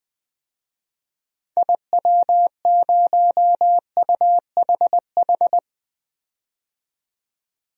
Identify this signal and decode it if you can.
Morse code, heard as a keyed tone, about 21 wpm, 700 Hz. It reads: IW0UHH
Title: DXCC-100 Call Signs - 6 Characters - 20wpm